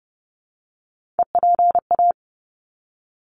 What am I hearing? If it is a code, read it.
Morse code, heard as a keyed tone, about 30 wpm, 700 Hz. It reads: EPA
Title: ARRL Sections 30wpm